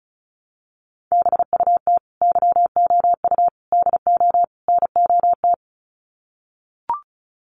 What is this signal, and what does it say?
Morse code, heard as a keyed tone, about 34 wpm, 700 Hz. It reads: BUT YOU DO NOT
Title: Sets of 4 Words 35wpm